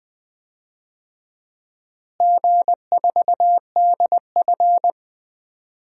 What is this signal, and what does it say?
Morse code, heard as a keyed tone, about 20 wpm, 700 Hz. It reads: G4DF
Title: DXCC-100 Call Signs - 4 Characters - Repeated 2x - 20wpm